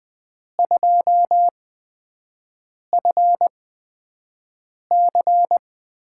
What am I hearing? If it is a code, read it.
Morse code, heard as a keyed tone, about 20 wpm, spaced as for 5 wpm, 700 Hz. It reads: 2FC
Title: USA Call Signs - 6 Characters - 8x Character Spacing - 20wpm